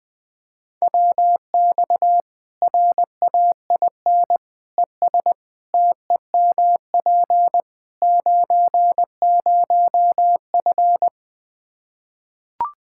Morse code, heard as a keyed tone, about 20 wpm, 700 Hz. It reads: WX RAIN ES TEMP 90F
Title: Practice QSO - Exchange 2 - Info - 20wpm